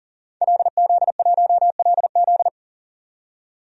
Morse code, heard as a keyed tone, about 40 wpm, 700 Hz. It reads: LZ1LZ
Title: DXCC-100 Call Signs - 5 Characters - 40wpm